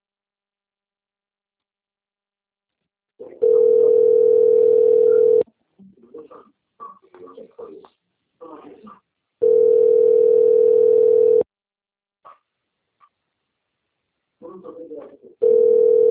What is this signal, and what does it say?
Ringback tone (North America), 440 + 480 Hz, with speech.
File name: out- - - - - .10005